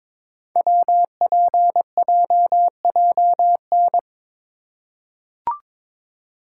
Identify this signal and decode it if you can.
Morse code, heard as a keyed tone, about 22 wpm, 700 Hz. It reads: WPJJN
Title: Code Groups - 5 Letters - 22wpm